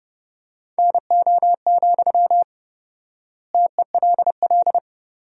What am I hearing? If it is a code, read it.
Morse code, heard as a keyed tone, about 30 wpm, 700 Hz. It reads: NO, TELL
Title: 4x Word-Spacing - Sentences from Top 200 Words - 30wpm